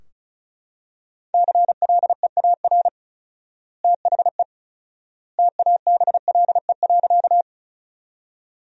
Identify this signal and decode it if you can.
Morse code, heard as a keyed tone, about 36 wpm, 700 Hz. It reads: CLEAR THE TABLE.
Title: 4x Word-Spacing - Sentences from Top 500 Words - 35wpm